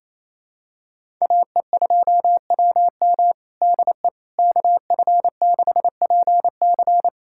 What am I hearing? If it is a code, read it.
Morse code, heard as a keyed tone, about 28 wpm, 700 Hz. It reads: AE2WM DE KF6PC
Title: USA Callsign DE Callsign - 5 Characters - 28wpm